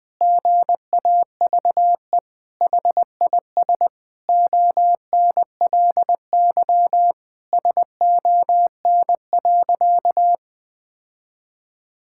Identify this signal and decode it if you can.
Morse code, heard as a keyed tone, about 19 wpm, 700 Hz. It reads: GAVE HIS ONLY SON.
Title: Sentences with 8-Word Phrases - Spoken Beginning - 20wpm